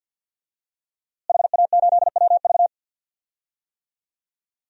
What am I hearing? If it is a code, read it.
Morse code, heard as a keyed tone, about 52 wpm, 700 Hz. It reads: HA8WV
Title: DXCC-100 Call Signs - 5 Characters - Repeated 2x - 50wpm